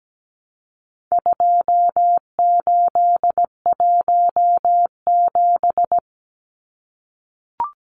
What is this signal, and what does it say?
Morse code, heard as a keyed tone, about 17 wpm, 700 Hz. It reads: 2817
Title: Numbers - 4-Digit 17wpm